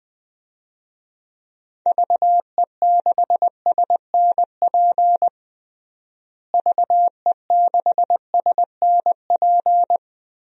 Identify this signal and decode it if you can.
Morse code, heard as a keyed tone, about 20 wpm, 700 Hz. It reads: VE6SNP VE6SNP
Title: DXCC-100 Call Signs - 6 Characters - Repeated 2x - 20wpm